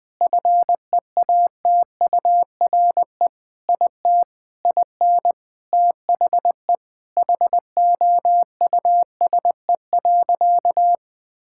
Morse code, heard as a keyed tone, about 20 wpm, 700 Hz. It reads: FEATURE IT IN THE HOUSE.
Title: Sentences from Top 1000 Words - Review Part 5 - 20wpm